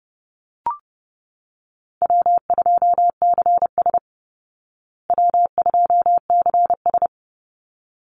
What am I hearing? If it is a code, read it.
Morse code, heard as a keyed tone, about 30 wpm, 700 Hz. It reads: W2CS W2CS
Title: Easy Call Signs Repeated 2x - 30wpm